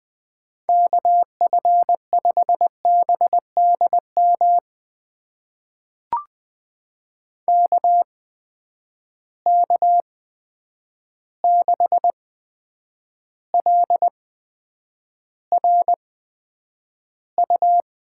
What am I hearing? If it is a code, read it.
Morse code, heard as a keyed tone, about 20 wpm, spaced as for 9 wpm, 700 Hz. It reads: KF5BDM KK6LRU
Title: USA Call Signs - 6 Characters - 8x Character Spacing - 20wpm